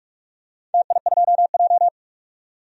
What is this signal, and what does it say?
Morse code, heard as a keyed tone, about 42 wpm, 700 Hz. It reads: TI2J
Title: DXCC-100 Call Signs - 4 Characters - 45wpm